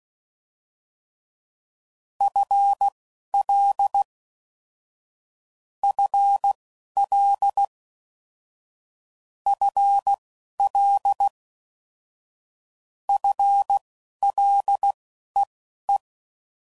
Morse code, heard as a keyed tone, about 16 wpm, spaced as for 11 wpm, 800 Hz. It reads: FL FL FL FLEE
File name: Basic Sets / QST de WA7RSO / Combnations